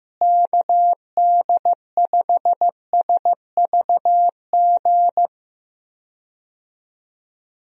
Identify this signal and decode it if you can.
Morse code, heard as a keyed tone, about 16 wpm, 700 Hz. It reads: KD5SVG
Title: USA Call Signs - 6 Characters - Repeated 2x - 15wpm